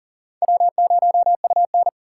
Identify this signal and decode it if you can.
Morse code, heard as a keyed tone, about 42 wpm, 700 Hz. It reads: W0UN
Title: USA Call Signs - 4 Characters - 8x Character Spacing - 40wpm